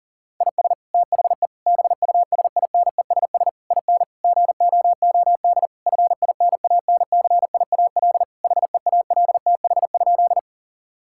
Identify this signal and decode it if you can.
Morse code, heard as a keyed tone, about 40 wpm, 700 Hz. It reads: IS THE BUSINESS IN GOOD FINANCIAL HEALTH?
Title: Sentences from Top 1000 Words - Review Part 5 - 40wpm